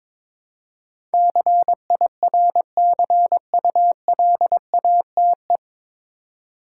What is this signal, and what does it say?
Morse code, heard as a keyed tone, about 22 wpm, 700 Hz. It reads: CIRCULATE
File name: 9 Letter Words - First 3 Letters Given - 22wpm